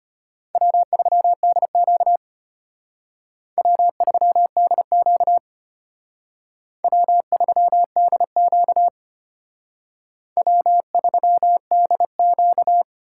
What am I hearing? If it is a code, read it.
Morse code, heard as a keyed tone, about 32 wpm, 700 Hz. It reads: W3DQ W3DQ W3DQ W3DQ